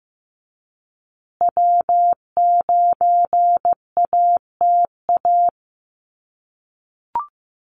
Morse code, heard as a keyed tone, about 15 wpm, 700 Hz. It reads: W9ATA
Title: USA Call Signs Repeated 3x 15wpm